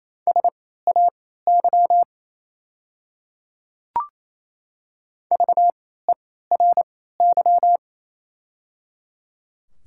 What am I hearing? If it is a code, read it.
Morse code, heard as a keyed tone, about 28 wpm, spaced as for 16 wpm, 700 Hz. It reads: SAY VERY
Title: 3x Character Spacing - Top 200 Words - 28wpm